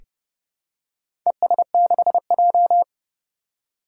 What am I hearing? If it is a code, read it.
Morse code, heard as a keyed tone, about 30 wpm, 700 Hz. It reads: ES6J
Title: Four Challenging Letter-Number Combinations 30wpm